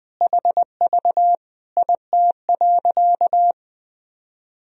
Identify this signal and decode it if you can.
Morse code, heard as a keyed tone, about 20 wpm, 700 Hz. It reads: HV IT.
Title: Sentences from Top 100 Words and Abbreviations 20wpm